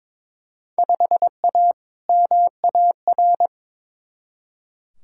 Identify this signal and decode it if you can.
Morse code, heard as a keyed tone, about 22 wpm, 700 Hz. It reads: 5A MAR
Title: Field Day Exchanges Encore - 22wpm